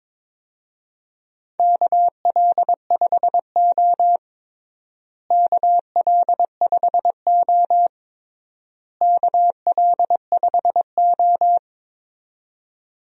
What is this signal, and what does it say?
Morse code, heard as a keyed tone, about 22 wpm, 700 Hz. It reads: KL5O KL5O KL5O
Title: USA Call Signs - 4 Characters - Repeated 3x - 22wpm